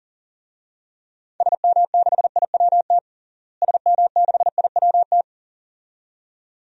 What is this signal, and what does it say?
Morse code, heard as a keyed tone, about 40 wpm, 700 Hz. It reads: SM6IWT SM6IWT
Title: DXCC-100 Call Signs - 6 Characters - Repeated 2x - 40wpm